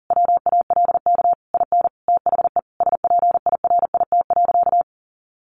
Morse code, heard as a keyed tone, about 37 wpm, 700 Hz. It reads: WALK IN THE SPIRIT.
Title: Sentences from Top 1400 Words 40wpm